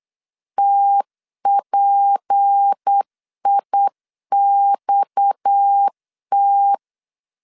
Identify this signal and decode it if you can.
Morse code, heard as a keyed tone, about 8 wpm, 800 Hz. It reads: TPIXT
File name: CW / DET / 1 9 1500C S C 8wpm 5653